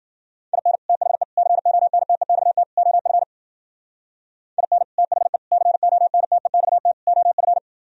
Morse code, heard as a keyed tone, about 51 wpm, 700 Hz. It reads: IN THE CONTEXT OF IN THE CONTEXT OF